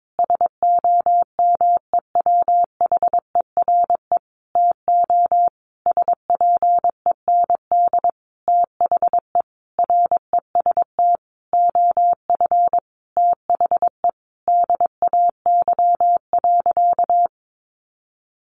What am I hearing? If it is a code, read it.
Morse code, heard as a keyed tone, about 22 wpm, 700 Hz. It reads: SOMEWHERE TO SPEND THE REST OF THE DAY.
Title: Sentences from Top 1400 Words 22wpm